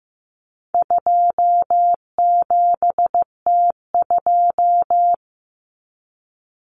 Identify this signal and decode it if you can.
Morse code, heard as a keyed tone, about 15 wpm, 700 Hz. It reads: 27T2
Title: Common Cut Numbers - 4-Digit 15wpm